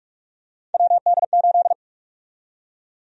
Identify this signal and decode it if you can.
Morse code, heard as a keyed tone, about 46 wpm, 700 Hz. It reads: WD8
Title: USA Call Sign Prefixes - 45wpm